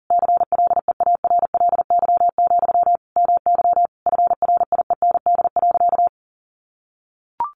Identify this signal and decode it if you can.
Morse code, heard as a keyed tone, about 40 wpm, 700 Hz. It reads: CLEARLY, MY FRIEND.
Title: Sentences from Top 1300 Words 40wpm